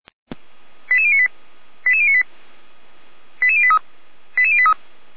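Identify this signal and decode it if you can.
Five-tone selective call (ZVEI), heard as quick runs of five tones, tones of about 70 ms. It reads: 88098 88098 88083 88083
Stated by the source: ZVEI five-tone call 88098 received with an SDR.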